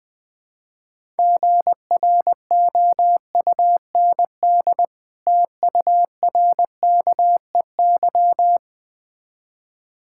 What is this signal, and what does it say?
Morse code, heard as a keyed tone, about 20 wpm, 690 Hz. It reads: GROUND TURKEY
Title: Difficult Words with Hints - 20wpm